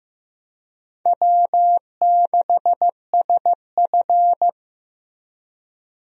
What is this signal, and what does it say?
Morse code, heard as a keyed tone, about 15 wpm, 700 Hz. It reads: W6SF
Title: USA Call Signs - 4 Characters - Repeated 3x - 15wpm